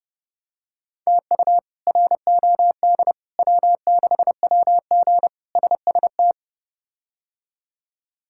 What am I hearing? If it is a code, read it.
Morse code, heard as a keyed tone, about 30 wpm, 700 Hz. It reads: TU ROD W6WG SST
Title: K1USN SST QSO Practice - 30wpm